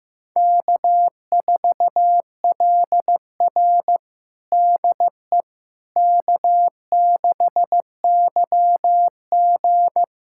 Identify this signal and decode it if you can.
Morse code, heard as a keyed tone, about 16 wpm, 700 Hz. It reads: K4LR DE K6YG
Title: USA Callsign DE Callsign - 4 Characters - 15wpm